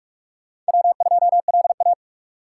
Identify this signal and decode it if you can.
Morse code, heard as a keyed tone, about 58 wpm, spaced as for 51 wpm, 700 Hz. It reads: W2LA
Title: Speed Racing - USA 1x2 Call Signs 45wpm